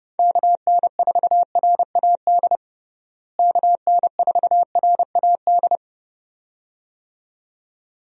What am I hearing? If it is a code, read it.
Morse code, heard as a keyed tone, about 30 wpm, 700 Hz. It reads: KN4RAD KN4RAD
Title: USA Call Signs - 6 Characters - Repeated 2x - 30wpm